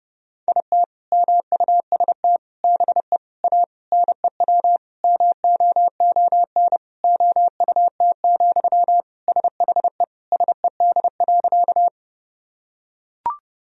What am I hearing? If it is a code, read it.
Morse code, heard as a keyed tone, about 30 wpm, 700 Hz. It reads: IT MUST BE A NEW MOON OUT, SHE SED.
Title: Sentences from Top 500 Words and Abbreviations 30wpm